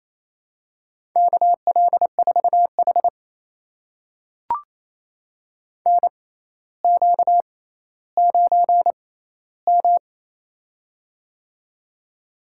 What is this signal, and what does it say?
Morse code, heard as a keyed tone, about 28 wpm, spaced as for 14 wpm, 700 Hz. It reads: KL4H NQ9M
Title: USA Call Signs - 4 Characters - 6x Character Spacing - 28wpm